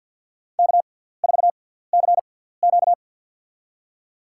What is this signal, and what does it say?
Morse code, heard as a keyed tone, about 53 wpm, spaced as for 17 wpm, 700 Hz. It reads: K4CQ
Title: USA Call Signs - 4 Characters - 6x Character Spacing - 50wpm